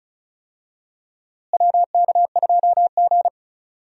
Morse code, heard as a keyed tone, about 36 wpm, 700 Hz. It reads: WK2G
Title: USA Call Signs - 4 Characters - Repeated 3x - 35wpm